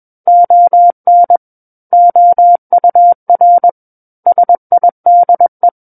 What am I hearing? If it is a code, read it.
Morse code, heard as a keyed tone, about 21 wpm, 700 Hz. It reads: ON OUR SIDE
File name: Book One / 12-Chapter 12 - 21 WPM